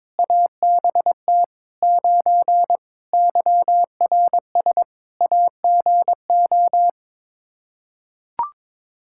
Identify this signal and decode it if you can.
Morse code, heard as a keyed tone, about 22 wpm, 700 Hz. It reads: ABT 9 YRS AGO
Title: Practice QSO - Exchange 3 - More Info - 22wpm